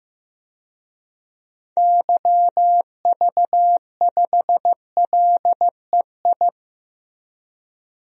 Morse code, heard as a keyed tone, about 15 wpm, 700 Hz. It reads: YV5LEI
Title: DXCC-100 Call Signs - 6 Characters - 15wpm